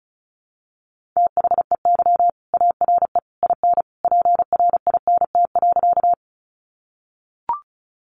Morse code, heard as a keyed tone, about 34 wpm, 700 Hz. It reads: THEY ARE IN PRINT.
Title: Sentences from Top 1400 Words 35wpm